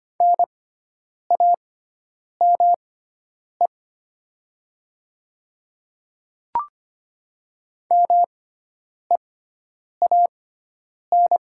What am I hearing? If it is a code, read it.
Morse code, heard as a keyed tone, about 25 wpm, spaced as for 9 wpm, 690 Hz. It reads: NAME MEAN
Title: 6x Character Spacing - Top 200 Words - 25wpm